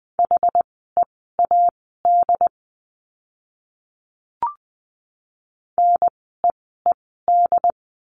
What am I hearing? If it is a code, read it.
Morse code, heard as a keyed tone, about 20 wpm, spaced as for 14 wpm, 700 Hz. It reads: HEAD NEED
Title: 068 Lesson 12f - Letter d - Words 20wpm